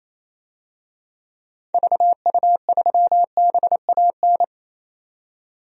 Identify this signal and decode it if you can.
Morse code, heard as a keyed tone, about 27 wpm, 700 Hz. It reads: VU3BAN
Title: DXCC-100 Call Signs - 6 Characters - 28wpm